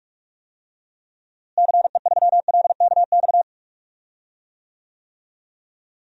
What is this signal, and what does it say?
Morse code, heard as a keyed tone, about 42 wpm, 700 Hz. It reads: CE3LKX
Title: DXCC-100 Call Signs - 6 Characters - Repeated 2x - 45wpm